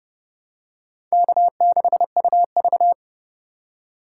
Morse code, heard as a keyed tone, about 27 wpm, 700 Hz. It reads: K6UV